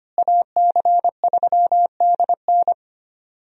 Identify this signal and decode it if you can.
Morse code, heard as a keyed tone, about 25 wpm, 700 Hz. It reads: AC3DN
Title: USA Call Signs - 5 Characters - Repeated 3x - 25wpm